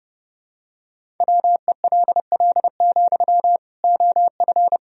Morse code, heard as a keyed tone, about 32 wpm, 700 Hz. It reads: WELL, OF
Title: Sentences from Top 1000 Words - Review Part 1 - 30wpm